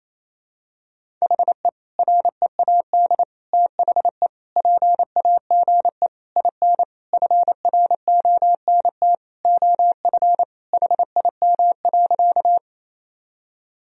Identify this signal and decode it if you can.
Morse code, heard as a keyed tone, about 28 wpm, 700 Hz. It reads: HE READ THE PAGE IN FRONT OF HIM.
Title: Sentences from Top 1000 Words - Review Part 1 - 28wpm